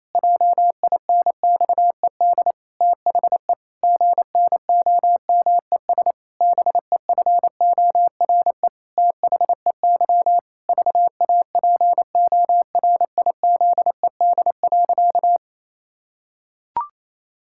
Morse code, heard as a keyed tone, about 28 wpm, 700 Hz. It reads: JINXED THE GNOMES BEFORE THEY VAPORIZED.
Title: Pangram Challenge v2 - 28wpm